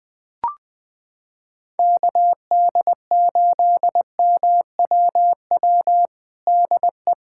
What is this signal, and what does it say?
Morse code, heard as a keyed tone, about 21 wpm, 700 Hz. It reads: KD8MWW DE
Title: USA Callsign DE Callsign - 6 Characters - 20wpm